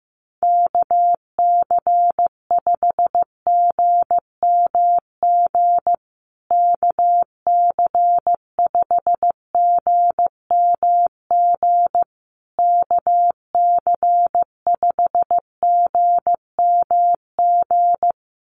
Morse code, heard as a keyed tone, about 15 wpm, 700 Hz. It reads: KC5GMG KC5GMG KC5GMG
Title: USA Call Signs Repeated 3x 15wpm